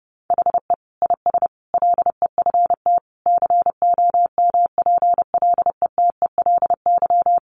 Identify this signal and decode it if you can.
Morse code, heard as a keyed tone, about 30 wpm, 700 Hz. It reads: HE IS LEFT COMPLETELY
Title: Sentences from Top 1100 Words 30wpm